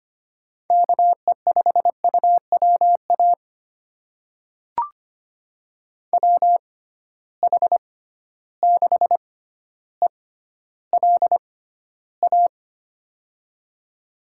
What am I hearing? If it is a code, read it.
Morse code, heard as a keyed tone, about 25 wpm, spaced as for 13 wpm, 700 Hz. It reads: KE5UWA WH6ELA
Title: USA Call Signs - 6 Characters - 6x Character Spacing - 25wpm